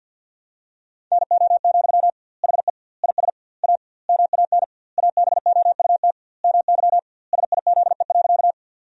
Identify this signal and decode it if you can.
Morse code, heard as a keyed tone, about 47 wpm, 700 Hz. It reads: NO, HE IS A MAN ABOUT MY SIZE.
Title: 2x Word-Spacing - Sentences from Top 400 Words - 50wpm